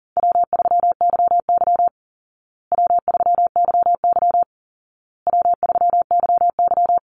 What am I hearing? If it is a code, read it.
Morse code, heard as a keyed tone, about 40 wpm, 700 Hz. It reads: W3YY W3YY W3YY
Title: Easy Call Signs Repeated 3x - 40wpm